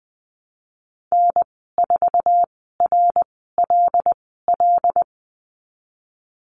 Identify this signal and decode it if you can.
Morse code, heard as a keyed tone, about 20 wpm, spaced as for 15 wpm, 700 Hz. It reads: N4RLL
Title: 088 Lesson 15g - Number 5 - Call Signs 20wpm